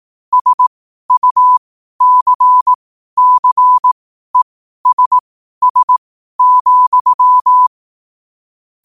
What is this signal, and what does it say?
Morse code, heard as a keyed tone, about 18 wpm, spaced as for 12 wpm, 1000 Hz. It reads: SUCCESS,
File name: qotdfiles / 12wpm